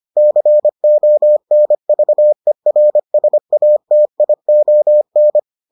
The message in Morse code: CONVERSATION

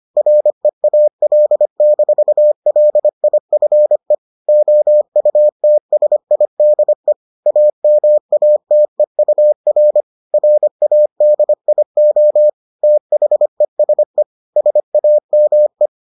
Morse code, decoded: REAL-LIFE OUTSIDE AMATEUR RADIO THESE SAME

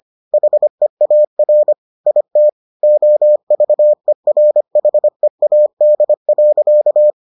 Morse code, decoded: HEAR IT OVERHEAD.

